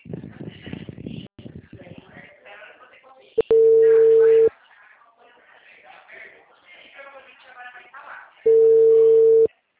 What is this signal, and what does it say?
Ringback tone (China, 450 Hz), with speech.
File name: out- -2049- - - .67894